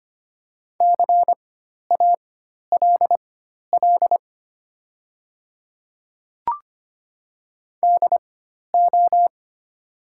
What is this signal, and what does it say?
Morse code, heard as a keyed tone, about 25 wpm, spaced as for 12 wpm, 700 Hz. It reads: CALL DO
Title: 4x Character Spacing - Top 100 Words - 25wpm